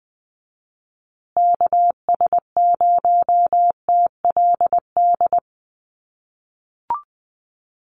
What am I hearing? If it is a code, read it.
Morse code, heard as a keyed tone, about 20 wpm, 700 Hz. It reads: KS0TLD
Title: USA Call Signs Repeated 3x 20wpm